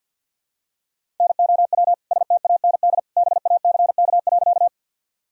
Morse code, heard as a keyed tone, about 47 wpm, 700 Hz. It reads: NOW STAND BACK.
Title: Sentences from Top 1000 Words - Review Part 1 - 50wpm